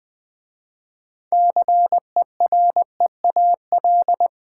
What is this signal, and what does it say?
Morse code, heard as a keyed tone, about 20 wpm, 700 Hz. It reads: CEREAL
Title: Easy Words with Hints - 20wpm